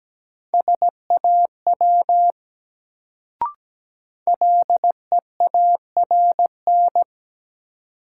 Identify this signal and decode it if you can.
Morse code, heard as a keyed tone, about 17 wpm, 700 Hz. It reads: SAW LEARN
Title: Sentences from Top 300 Words - New Word Drill - 17wpm